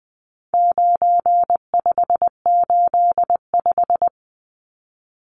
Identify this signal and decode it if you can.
Morse code, heard as a keyed tone, about 20 wpm, 700 Hz. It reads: 9585
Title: Numbers - 4-Digit 20wpm